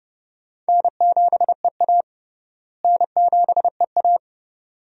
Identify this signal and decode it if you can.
Morse code, heard as a keyed tone, about 30 wpm, 700 Hz. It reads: N7EA N7EA
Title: USA Call Signs - 4 Characters - Repeated 2x - 30wpm